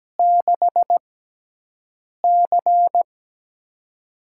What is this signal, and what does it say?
Morse code, heard as a keyed tone, about 17 wpm, spaced as for 6 wpm, 700 Hz. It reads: 6C
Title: USA Call Signs - 6x Character Spacing - 17wpm